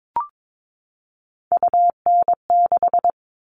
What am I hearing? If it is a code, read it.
Morse code, heard as a keyed tone, about 22 wpm, 700 Hz. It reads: UN6
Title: All Cut Numbers - 3-Digit 22wpm